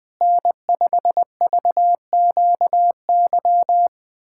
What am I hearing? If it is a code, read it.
Morse code, heard as a keyed tone, about 21 wpm, 700 Hz. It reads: N5VQY
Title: USA Callsign DE Callsign - 5 Characters - 20wpm